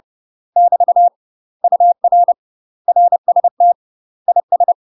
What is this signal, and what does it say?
Morse code, heard as a keyed tone, about 30 wpm, 700 Hz. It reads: <BT> UR RST IS
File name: UK MASTER QSOs TRAINING / 30WPM / UK MASTER QSOs 0015